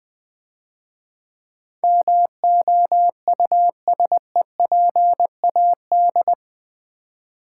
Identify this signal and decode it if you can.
Morse code, heard as a keyed tone, about 23 wpm, 700 Hz. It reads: MOUSEPAD
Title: Easy Words - Speed-Racing with Hints - 20wpm